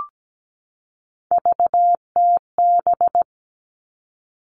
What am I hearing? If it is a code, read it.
Morse code, heard as a keyed tone, about 17 wpm, 700 Hz. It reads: VTB
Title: All Cut Numbers - 3-Digit 17wpm